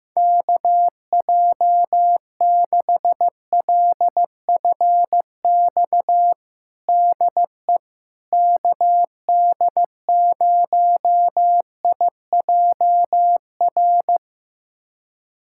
Morse code, heard as a keyed tone, about 15 wpm, 700 Hz. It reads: KJ6LFX DE KD0IJR
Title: USA Callsign DE Callsign - 6 Characters - 15wpm